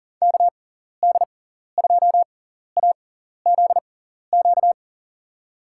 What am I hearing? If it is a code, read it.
Morse code, heard as a keyed tone, about 40 wpm, spaced as for 14 wpm, 700 Hz. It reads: KD2AZQ
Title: USA Call Signs - 6 Characters - 6x Character Spacing - 40wpm